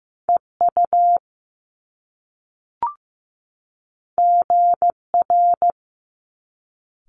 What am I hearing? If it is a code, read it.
Morse code, heard as a keyed tone, about 15 wpm, 700 Hz. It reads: EU GR
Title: Top Two Letter Combinations in Words 15wpm